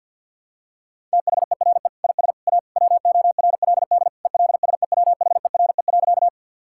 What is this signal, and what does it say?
Morse code, heard as a keyed tone, about 51 wpm, 700 Hz. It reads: THERE IS A WORLD ELSEWHERE.
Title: Sentences from Top 1600 Words 50wpm